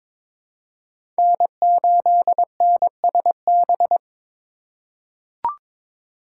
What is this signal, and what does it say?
Morse code, heard as a keyed tone, about 22 wpm, 700 Hz. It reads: N8NSB
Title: USA Call Signs - 5 Characters - 3x Character Spacing - 22wpm